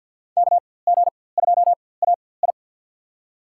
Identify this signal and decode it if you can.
Morse code, heard as a keyed tone, about 48 wpm, spaced as for 23 wpm, 700 Hz. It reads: KG2AI
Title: USA Call Signs - 5 Characters - 4x Character Spacing - 50wpm